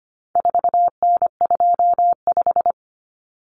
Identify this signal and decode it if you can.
Morse code, heard as a keyed tone, about 25 wpm, 700 Hz. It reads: 4N25
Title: Common Cut Numbers - 4-Digit 25wpm